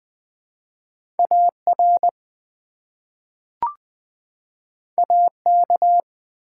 Morse code, heard as a keyed tone, about 20 wpm, 700 Hz. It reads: AR AK
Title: ARRL Sections v2 - 20wpm